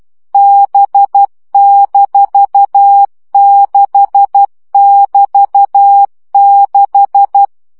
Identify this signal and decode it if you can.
Morse code, heard as a keyed tone, about 12 wpm, 800 Hz. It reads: B-6<BT>6
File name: CW gyakorlas b 6 egyenlo minusz 10 baud